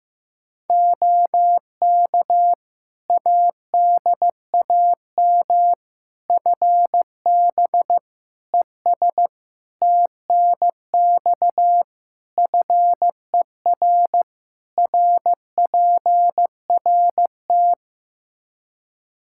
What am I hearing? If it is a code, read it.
Morse code, heard as a keyed tone, about 15 wpm, 700 Hz. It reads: OK ADAM FB ES TNX FER RPRT
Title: Practice QSO - Exchange 2 - Info - 15wpm